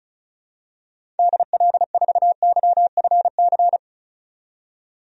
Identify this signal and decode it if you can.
Morse code, heard as a keyed tone, about 36 wpm, 700 Hz. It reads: DL4YFC
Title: DXCC-100 Call Signs - 6 Characters - 35wpm